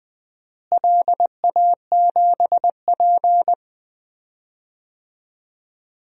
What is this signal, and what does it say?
Morse code, heard as a keyed tone, about 20 wpm, 700 Hz. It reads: LA7P
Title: DXCC-100 Call Signs - 4 Characters - Repeated 2x - 20wpm